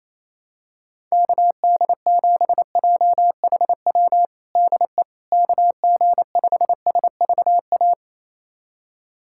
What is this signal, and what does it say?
Morse code, heard as a keyed tone, about 28 wpm, 700 Hz. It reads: KD7JHW DE KG5SVA
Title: USA Callsign DE Callsign - 6 Characters - 28wpm